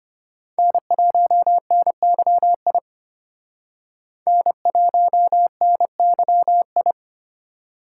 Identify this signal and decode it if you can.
Morse code, heard as a keyed tone, about 33 wpm, 700 Hz. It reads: N1NYI N1NYI